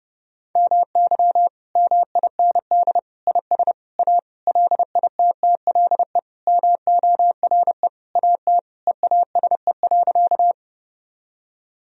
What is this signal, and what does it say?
Morse code, heard as a keyed tone, about 29 wpm, 700 Hz. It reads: MY MIND IS A LITTLE MORE AT EASE.
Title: Sentences from Top 1000 Words - Review Part 1 - 30wpm